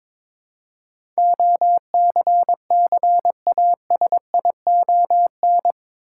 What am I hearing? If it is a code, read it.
Morse code, heard as a keyed tone, about 22 wpm, 700 Hz. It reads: OCCASION
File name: practice / 22wpm / Eight Letter Words - 22wpm